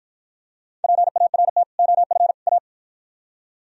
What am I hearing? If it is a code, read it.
Morse code, heard as a keyed tone, about 52 wpm, 700 Hz. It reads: PART OF A